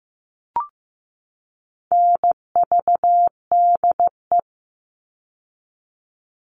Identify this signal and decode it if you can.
Morse code, heard as a keyed tone, about 15 wpm, 700 Hz. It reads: NVDE
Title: All Cut Numbers - 4-Digit 15wpm